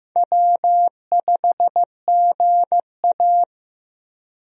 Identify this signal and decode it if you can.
Morse code, heard as a keyed tone, about 15 wpm, 700 Hz. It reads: W5GA
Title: USA Call Signs - 4 Characters - Repeated 2x - 15wpm